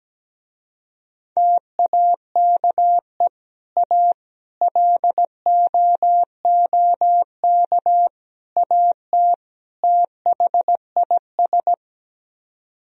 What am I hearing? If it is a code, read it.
Morse code, heard as a keyed tone, about 17 wpm, 700 Hz. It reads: TAKE A LOOK AT THIS
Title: Speed-Racing - Sets of 5 Words 17wpm